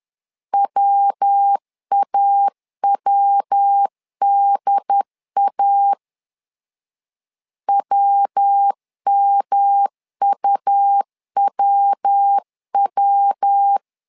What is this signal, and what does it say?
Morse code, heard as a keyed tone, about 10 wpm, 800 Hz. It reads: WAWDA WMUWW
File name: LEZIONI / 3 MWRO / 3 MW 10wpm 1959